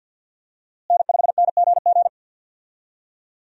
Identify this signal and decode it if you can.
Morse code, heard as a keyed tone, about 52 wpm, 690 Hz. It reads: N5NGG
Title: USA Call Signs - 5 Characters - Repeated 3x - 50wpm